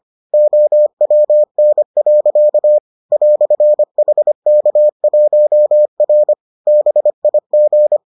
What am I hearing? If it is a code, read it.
Morse code, heard as a keyed tone, about 25 wpm, 600 Hz. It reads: OWN. "HK1R BIG